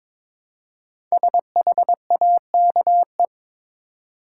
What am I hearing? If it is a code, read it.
Morse code, heard as a keyed tone, about 22 wpm, 700 Hz. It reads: SHAKE
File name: 5 Letter Words - First 3 Letters Given - 22wpm